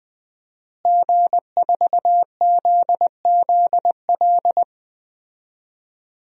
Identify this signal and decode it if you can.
Morse code, heard as a keyed tone, about 20 wpm, 700 Hz. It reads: G4ZZL
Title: DXCC-100 Call Signs - 5 Characters - Repeated 3x - 20wpm